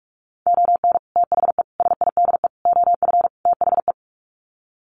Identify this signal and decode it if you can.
Morse code, heard as a keyed tone, about 44 wpm, 700 Hz. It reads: ON THE SIDE OF THE
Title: Sets of 5 Words 45wpm